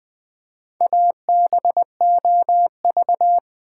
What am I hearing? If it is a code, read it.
Morse code, heard as a keyed tone, about 20 wpm, 700 Hz. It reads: ABOV
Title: Build A Word - 1 Letter At A Time - 5-Letter Words - 20wpm